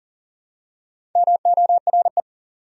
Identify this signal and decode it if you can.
Morse code, heard as a keyed tone, about 40 wpm, 700 Hz. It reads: MORE
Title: Top 100 Words 40wpm